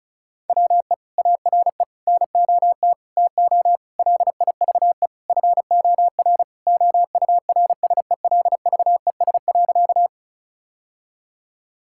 Morse code, heard as a keyed tone, about 36 wpm, 700 Hz. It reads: WE ARE NOT TO LIVE FOR OURSELVES.
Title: Sentences from Top 1600 Words 35wpm